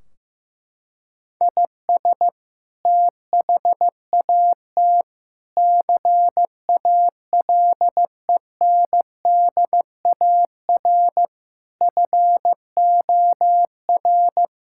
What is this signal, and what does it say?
Morse code, heard as a keyed tone, about 16 wpm, 700 Hz. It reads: IS THAT CALENDAR FOR
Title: Sentences from Top 1600 Words 15wpm